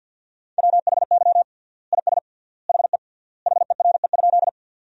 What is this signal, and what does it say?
Morse code, heard as a keyed tone, about 51 wpm, 700 Hz. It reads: WHY IS HE HERE?
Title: 3x Word-Spacing - Sentences from Top 200 Words - 50wpm